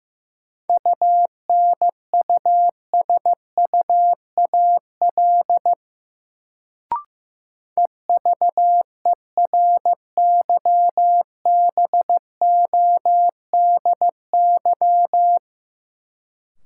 Morse code, heard as a keyed tone, about 15 wpm, 700 Hz. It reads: UNUSUAL EVERYBODY
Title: Sentences from Top 1400 Words - New Word Drill Alt - 15wpm